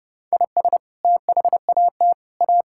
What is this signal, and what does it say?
Morse code, heard as a keyed tone, about 32 wpm, 700 Hz. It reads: IS THAT A
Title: Sentences from Top 1000 Words - Review Part 5 - 30wpm